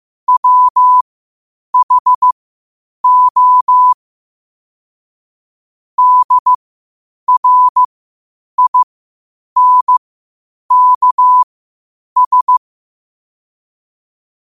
Morse code, heard as a keyed tone, about 15 wpm, spaced as for 8 wpm, 1000 Hz. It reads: WHO DRINKS
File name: qotdfiles / 8wpm